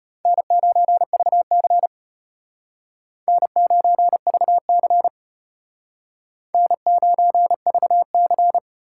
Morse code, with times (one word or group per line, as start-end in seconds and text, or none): 0.3-1.9 N9VC
3.3-5.1 N9VC
6.5-8.6 N9VC